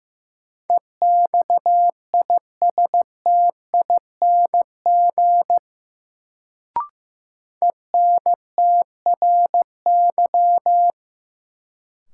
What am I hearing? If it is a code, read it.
Morse code, heard as a keyed tone, about 15 wpm, 700 Hz. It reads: EXISTING ENTRY
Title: Sentences from Top 1500 Words - New Word Drill - 15wpm